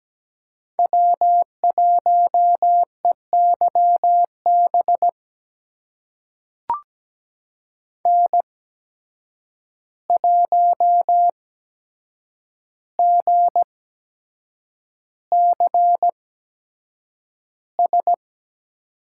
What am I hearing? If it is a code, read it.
Morse code, heard as a keyed tone, about 17 wpm, spaced as for 7 wpm, 700 Hz. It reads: W1EYB N1GCS
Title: USA Call Signs - 5 Characters - 8x Character Spacing - 17wpm